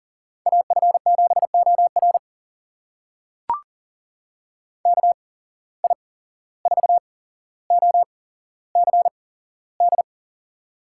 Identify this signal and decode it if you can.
Morse code, heard as a keyed tone, about 40 wpm, spaced as for 16 wpm, 700 Hz. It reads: AF7OR KI4OCD
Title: USA Call Signs - 8x Character Spacing - 40wpm